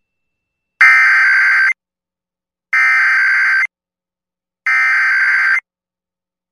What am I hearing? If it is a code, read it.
Emergency Alert System (SAME) tones (header bursts), heard as harsh buzzing bursts. It is heard from 0.8 to 5.6 s.